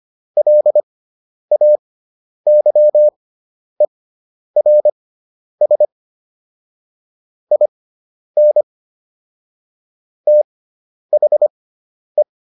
Morse code, heard as a keyed tone, about 25 wpm, spaced as for 10 wpm, 600 Hz. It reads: LAYERS IN THE